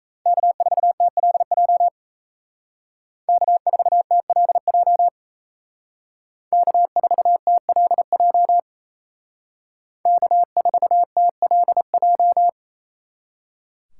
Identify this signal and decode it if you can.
Morse code, heard as a keyed tone, about 36 wpm, 700 Hz. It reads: K4TLJ K4TLJ K4TLJ K4TLJ